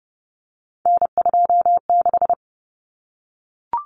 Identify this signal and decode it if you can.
Morse code, heard as a keyed tone, about 30 wpm, 700 Hz. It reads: N26
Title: Common Cut Numbers - 3-Digit 30wpm